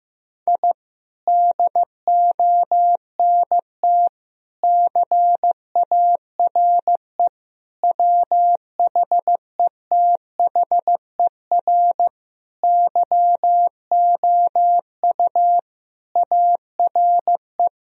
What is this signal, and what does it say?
Morse code, heard as a keyed tone, about 15 wpm, 700 Hz. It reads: I DONT CARE WHETHER YOU ARE
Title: Sentences from Top 1000 Words - Review Part 3 - 15wpm